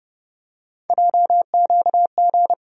Morse code, heard as a keyed tone, about 30 wpm, 700 Hz. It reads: JQG
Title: USA Call Sign Postfixes - 3 Letters 30wpm